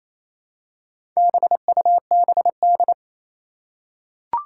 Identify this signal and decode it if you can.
Morse code, heard as a keyed tone, about 28 wpm, 700 Hz. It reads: BUBD
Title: Four Challenging Letter-Number Combinations 28wpm